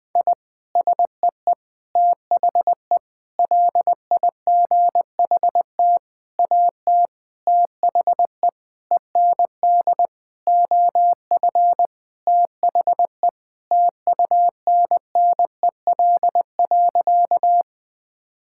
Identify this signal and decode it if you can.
Morse code, heard as a keyed tone, about 20 wpm, 700 Hz. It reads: I SEE THE LIGHT AT THE END OF THE TUNNEL.
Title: Sentences with 8-Word Phrases - 20wpm